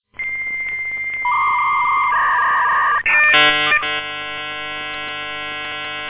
Dial-up modem handshake.